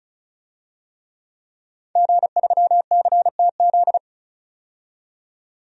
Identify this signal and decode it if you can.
Morse code, heard as a keyed tone, about 36 wpm, 700 Hz. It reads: G3CTZ
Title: DXCC-100 Call Signs - 5 Characters - 35wpm